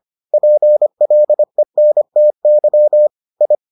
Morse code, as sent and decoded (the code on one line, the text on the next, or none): .--. .-.. -. - -.-- / ..
PLENTY I